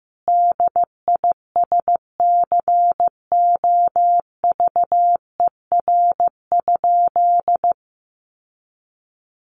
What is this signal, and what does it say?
Morse code, heard as a keyed tone, about 15 wpm, 700 Hz. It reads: DISCOVER?
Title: Sentences from Top 1100 Words 15wpm